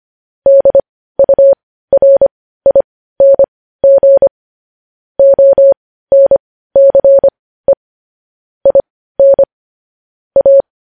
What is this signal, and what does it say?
Morse code, heard as a keyed tone, about 25 wpm, spaced as for 15 wpm, 550 Hz. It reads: DURING ONCE IN A